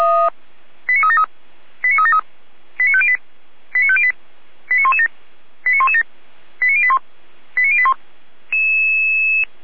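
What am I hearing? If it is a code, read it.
Five-tone selective call (ZVEI), heard as quick runs of five tones, tones of about 70 ms. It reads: 89373 89373 89508 89508 89117 89117 89081 89081
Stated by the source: Five-tone ZVEI selective call 89092 received by SDR, 455 channel.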